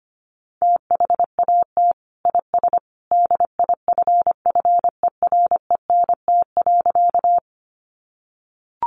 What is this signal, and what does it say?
Morse code, heard as a keyed tone, about 25 wpm, 700 Hz. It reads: THAT IS DIFFERENT.
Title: Sentences from Top 600 Words 25wpm